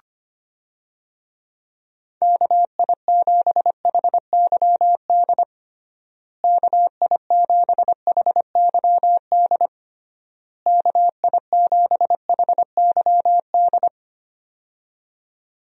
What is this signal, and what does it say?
Morse code, heard as a keyed tone, about 25 wpm, 700 Hz. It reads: KI7HYD KI7HYD KI7HYD
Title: USA Call Signs - 6 Characters - Repeated 3x - 25wpm